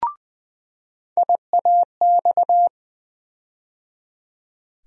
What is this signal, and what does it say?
Morse code, heard as a keyed tone, about 21 wpm, 700 Hz. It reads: IAX
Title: USA Call Sign Postfixes - 3 Letters 20wpm